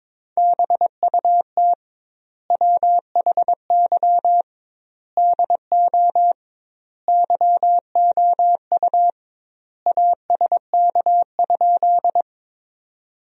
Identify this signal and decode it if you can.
Morse code, heard as a keyed tone, about 22 wpm, 700 Hz. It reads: BUT WHY DO YOU ASK?
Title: 2x Word-Spacing - Sentences from Top 200 Words - 22wpm